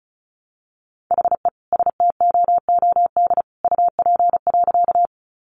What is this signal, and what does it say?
Morse code, heard as a keyed tone, about 34 wpm, 700 Hz. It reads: HE STOOD UP.
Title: Sentences from Top 500 Words 35wpm